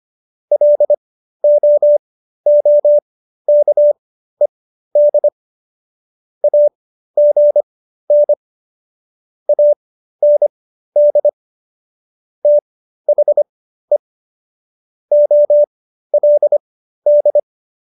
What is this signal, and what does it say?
Morse code, heard as a keyed tone, about 25 wpm, spaced as for 13 wpm, 590 Hz. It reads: LOOKED AGN AND THE OLD